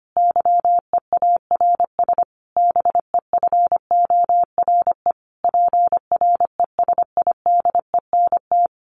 Morse code, heard as a keyed tone, about 25 wpm, 700 Hz. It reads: YEARS BEFORE PRESIDENT
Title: A Sound Of Thunder 25wpm